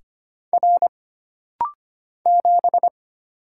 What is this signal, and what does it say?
Morse code, heard as a keyed tone, about 25 wpm, 700 Hz. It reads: R 7
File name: Single Character (Letter:Number:Punctuation:Prosign) - Rapid-Fire - 25wpm